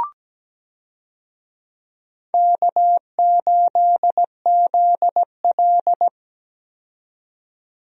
Morse code, heard as a keyed tone, about 17 wpm, 700 Hz. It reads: K8ZL
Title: USA Call Signs - 4 Characters - 17wpm